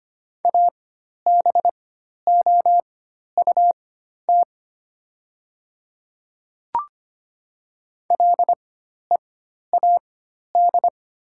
Morse code, heard as a keyed tone, about 25 wpm, spaced as for 12 wpm, 690 Hz. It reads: ABOUT LEAD
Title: 4x Character Spacing - Top 500 Words - 25wpm